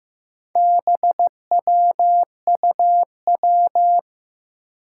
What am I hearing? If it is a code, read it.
Morse code, heard as a keyed tone, about 15 wpm, 700 Hz. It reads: BWUW
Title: Four Challenging Letter-Number Combinations 15wpm